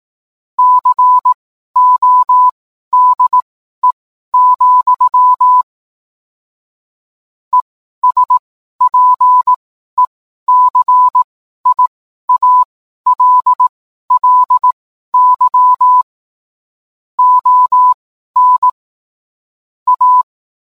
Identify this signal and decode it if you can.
Morse code, heard as a keyed tone, about 18 wpm, spaced as for 12 wpm, 1000 Hz. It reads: CODE, ESPECIALLY ON A
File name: cwblog / cw mp3 directory / mem morse 12